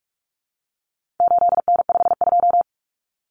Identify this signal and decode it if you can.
Morse code, heard as a keyed tone, about 44 wpm, 700 Hz. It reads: 8N52
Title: Common Cut Numbers - 4-Digit 45wpm